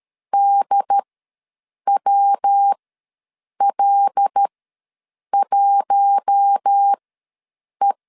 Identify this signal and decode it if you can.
Morse code, heard as a keyed tone, about 13 wpm, spaced as for 7 wpm, 800 Hz. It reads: DWL1E
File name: LEZIONI / 5 LBVC / 5 L ws 12wpm 2958